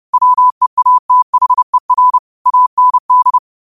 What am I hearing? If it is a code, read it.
Morse code, heard as a keyed tone, about 30 wpm, 1000 Hz. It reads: WEATHER AND
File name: qotdfiles / 30wpm